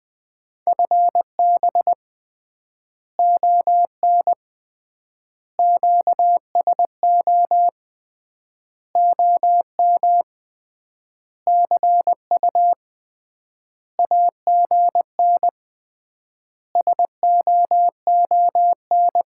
Morse code, heard as a keyed tone, about 20 wpm, 700 Hz. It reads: FB ON QSO OM CU AGN SOON
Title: Common QSO Phrases - 3x Word-Spacing - 20wpm